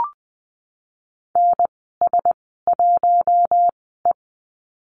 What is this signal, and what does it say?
Morse code, heard as a keyed tone, about 19 wpm, 700 Hz. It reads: N S 1 E
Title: 055 Lesson 10g - Letter r - Call Signs 20wpm